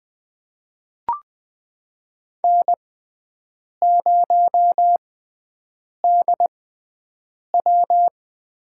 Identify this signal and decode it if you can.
Morse code, heard as a keyed tone, about 20 wpm, spaced as for 7 wpm, 700 Hz. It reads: N0DW
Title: USA Call Signs - 5 Characters - 6x Character Spacing - 20wpm